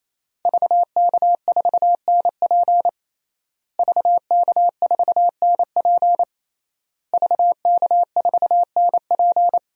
Morse code, heard as a keyed tone, about 28 wpm, 700 Hz. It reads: VK4NP VK4NP VK4NP
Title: DXCC-100 Call Signs - 5 Characters - Repeated 3x - 28wpm